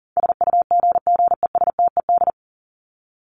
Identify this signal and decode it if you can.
Morse code, heard as a keyed tone, about 40 wpm, 700 Hz. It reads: SUGGESTED
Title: ED Suffix - Top 100 Words with Base Words 40wpm